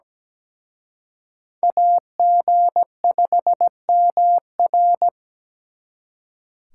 Morse code, heard as a keyed tone, about 20 wpm, 700 Hz. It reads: AG5MR
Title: Speed Racing - USA 2x2 Call Signs 17wpm